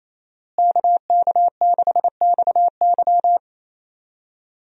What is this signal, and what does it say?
Morse code, heard as a keyed tone, about 28 wpm, 700 Hz. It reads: KK6XY
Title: USA Call Signs - 5 Characters - Repeated 2x - 28wpm